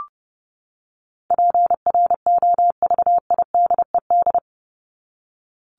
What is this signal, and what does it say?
Morse code, heard as a keyed tone, about 30 wpm, 700 Hz. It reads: PROVIDED
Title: Top 1751-2000 Words 30wpm